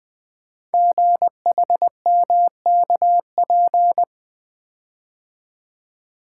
Morse code, heard as a keyed tone, about 20 wpm, 700 Hz. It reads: GHMKP
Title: Code Groups - 5 Letters - 20wpm